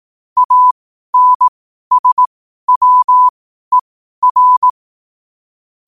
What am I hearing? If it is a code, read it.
Morse code, heard as a keyed tone, about 18 wpm, spaced as for 12 wpm, 1000 Hz. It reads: ANSWER
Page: New quotes every day in morse code at 12 Words per minute.